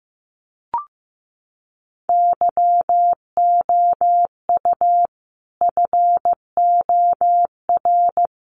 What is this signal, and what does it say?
Morse code, heard as a keyed tone, about 15 wpm, 700 Hz. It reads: YOU FOR
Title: Sets of 5 Words 15wpm